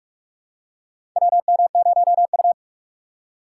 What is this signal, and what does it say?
Morse code, heard as a keyed tone, about 46 wpm, 700 Hz. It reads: WM0U
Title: USA Call Signs - 4 Characters - 4x Character Spacing - 45wpm